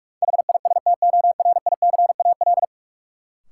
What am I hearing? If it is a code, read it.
Morse code, heard as a keyed tone, about 41 wpm, 700 Hz. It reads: HISTORICAL
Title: Sentences from Top 600 Words - New Word Drill - 45wpm